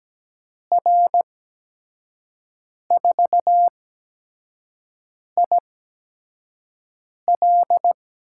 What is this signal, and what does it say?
Morse code, heard as a keyed tone, about 17 wpm, spaced as for 5 wpm, 700 Hz. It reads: R4IL
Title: USA Call Signs - 8x Character Spacing - 17wpm